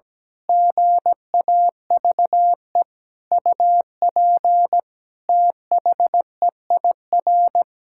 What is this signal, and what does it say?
Morse code, heard as a keyed tone, about 17 wpm, 700 Hz. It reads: GAVE UP THEIR
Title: Pangram Challenge v2 - 17wpm